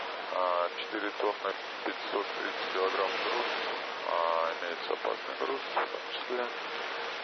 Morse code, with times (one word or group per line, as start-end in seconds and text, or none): none